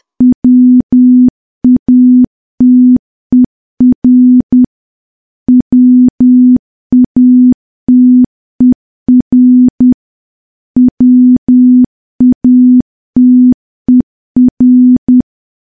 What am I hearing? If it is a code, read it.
Morse code, heard as a keyed tone, about 10 wpm, 250 Hz. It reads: WATER WATER WATER
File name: morse / morse-training-10WPM